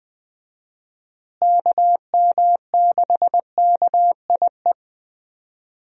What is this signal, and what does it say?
Morse code, heard as a keyed tone, about 20 wpm, 700 Hz. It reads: KM6KIE
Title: USA Call Signs - 6 Characters - Repeated 3x - 20wpm